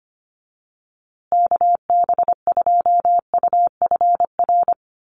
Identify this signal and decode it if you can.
Morse code, heard as a keyed tone, about 25 wpm, 700 Hz. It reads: KB2UFR
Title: USA Call Signs Repeated 3x 25wpm